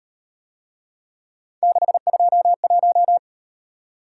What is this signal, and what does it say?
Morse code, heard as a keyed tone, about 38 wpm, 700 Hz. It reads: 621